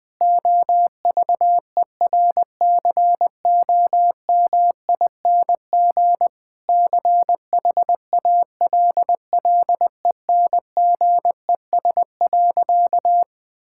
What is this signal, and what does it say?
Morse code, heard as a keyed tone, about 20 wpm, 700 Hz. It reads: OVERCOMING CHALLENGES.
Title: Difficult Sentences with Hints - 20wpm